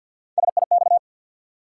Morse code, heard as a keyed tone, about 53 wpm, 700 Hz. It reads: SIX
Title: Sentences from Top 600 Words - New Word Drill - 50wpm